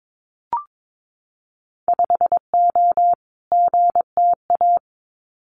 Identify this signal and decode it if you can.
Morse code, heard as a keyed tone, about 22 wpm, 700 Hz. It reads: 5O GTA
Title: Winter Field Day Exchanges Encore 22wpm